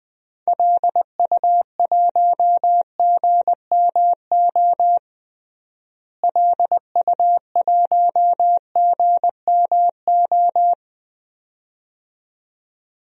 Morse code, heard as a keyed tone, about 20 wpm, 700 Hz. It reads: LU1GMO LU1GMO
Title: DXCC-100 Call Signs - 6 Characters - Repeated 2x - 20wpm